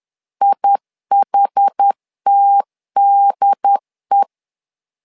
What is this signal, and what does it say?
Morse code, heard as a keyed tone, about 10 wpm, 800 Hz. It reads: IHTDE
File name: LEZIONI / 2 UHDN / 2 UHD 10wpm 1826